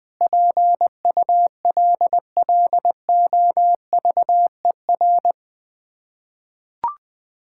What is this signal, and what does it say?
Morse code, heard as a keyed tone, about 20 wpm, 700 Hz. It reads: PULLOVER
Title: Difficult Words with Hints - 20wpm